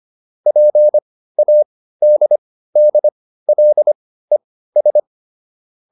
Morse code, heard as a keyed tone, about 25 wpm, spaced as for 15 wpm, 600 Hz. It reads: PADDLES